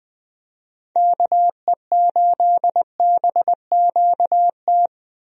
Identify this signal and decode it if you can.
Morse code, heard as a keyed tone, about 21 wpm, 700 Hz. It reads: KE8BQT
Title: USA Call Signs - 6 Characters - Repeated 2x - 20wpm